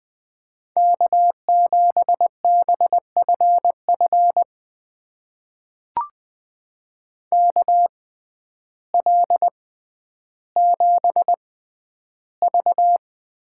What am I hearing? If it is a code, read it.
Morse code, heard as a keyed tone, about 20 wpm, spaced as for 11 wpm, 700 Hz. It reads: K7BFF KL7V
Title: USA Call Signs - 5 Characters - 6x Character Spacing - 20wpm